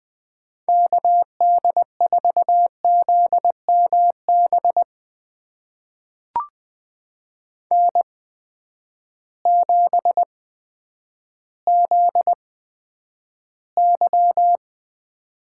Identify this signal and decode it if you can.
Morse code, heard as a keyed tone, about 20 wpm, spaced as for 10 wpm, 700 Hz. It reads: KD4ZMB N7ZY
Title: USA Call Signs - 8x Character Spacing - 20wpm